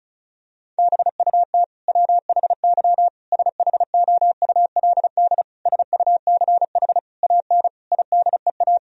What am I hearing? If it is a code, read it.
Morse code, heard as a keyed tone, about 34 wpm, 700 Hz. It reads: BUT WHY SHOULD SUCH AN IDEA
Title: Sentences from Top 1000 Words - Review Part 1 - 35wpm